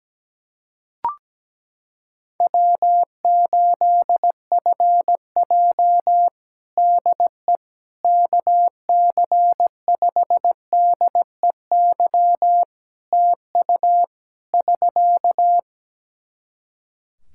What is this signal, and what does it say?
Morse code, heard as a keyed tone, about 17 wpm, 700 Hz. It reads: W8FJ DE KC5DEY TU <SK>
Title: Practice QSO - Exchange 4 - Ending - 17wpm